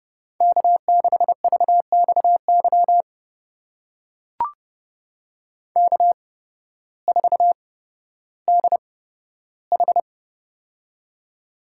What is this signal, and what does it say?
Morse code, heard as a keyed tone, about 30 wpm, spaced as for 14 wpm, 700 Hz. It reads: K6VXY K4DH
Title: USA Call Signs - 8x Character Spacing - 30wpm